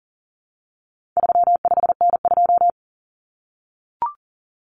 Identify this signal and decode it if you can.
Morse code, heard as a keyed tone, about 40 wpm, 700 Hz. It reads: 35N2
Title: Common Cut Numbers - 4-Digit 40wpm